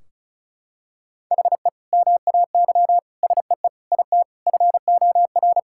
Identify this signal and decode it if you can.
Morse code, heard as a keyed tone, about 36 wpm, 700 Hz. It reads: HE MAY SEE IT FOR
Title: Sentences from Top 1600 Words 35wpm